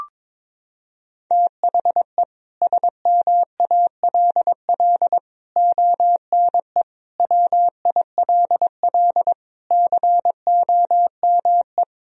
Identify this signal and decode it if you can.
Morse code, heard as a keyed tone, about 22 wpm, 700 Hz. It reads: THE SMALL ONE WILL COME
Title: Sentences from Top 1000 Words - Review Part 1 - 22wpm